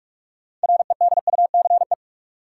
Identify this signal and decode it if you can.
Morse code, heard as a keyed tone, about 41 wpm, 700 Hz. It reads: REDUCE
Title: Sentences from Top 1100 Words - New Word Drill - 45wpm